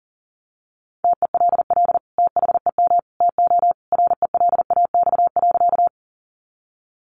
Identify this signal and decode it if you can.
Morse code, heard as a keyed tone, about 40 wpm, 700 Hz. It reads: TELL THEM TO RELAX.
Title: Sentences from Top 1400 Words 40wpm